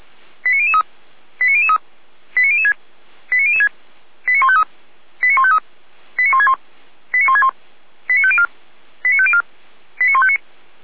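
Five-tone selective call (ZVEI), heard as quick runs of five tones, tones of about 70 ms. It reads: 89003 89003 89006 89006 89153 89153 89161 89161 89504 89504 89159
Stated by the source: Five-tone ZVEI selective call 89165 received by SDR, channel 455.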